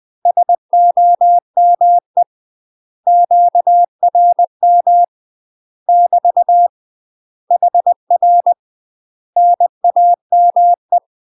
Morse code, heard as a keyed tone, about 20 wpm, 700 Hz. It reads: SOME QRM <BT> HR NAME
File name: UK MASTER QSOs TRAINING / 20WPM / UK MASTER QSOs 0008